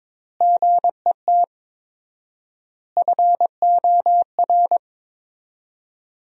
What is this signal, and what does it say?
Morse code, heard as a keyed tone, about 21 wpm, 700 Hz. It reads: GET FOR
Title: 4x Word-Spacing - Sentences from Top 400 Words - 22wpm